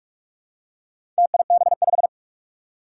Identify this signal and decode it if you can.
Morse code, heard as a keyed tone, about 46 wpm, 700 Hz. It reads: TIB5
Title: Four Challenging Letter-Number Combinations 45wpm